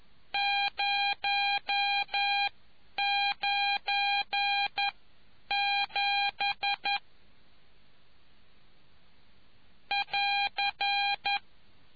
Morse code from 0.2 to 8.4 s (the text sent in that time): 097